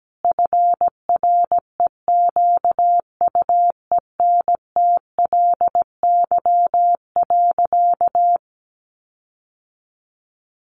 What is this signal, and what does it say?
Morse code, heard as a keyed tone, about 17 wpm, 700 Hz. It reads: FREQUENTLY.
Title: Sentences from Top 1300 Words 17wpm